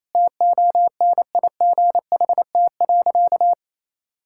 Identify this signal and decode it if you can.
Morse code, heard as a keyed tone, about 28 wpm, 700 Hz. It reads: TONIGHT.
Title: Sentences from Top 1000 Words - Review Part 6 - 28wpm